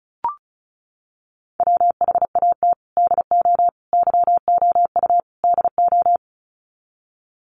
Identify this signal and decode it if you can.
Morse code, heard as a keyed tone, about 34 wpm, 700 Hz. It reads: WHAT DO YOU DO
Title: Sets of 4 Words 35wpm